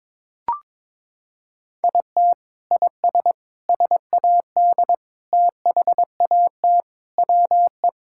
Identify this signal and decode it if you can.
Morse code, heard as a keyed tone, about 22 wpm, 700 Hz. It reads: IT IS SAD THAT WE
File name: practice / 22wpm / Sentences with 4-Word Phrases - 22wpm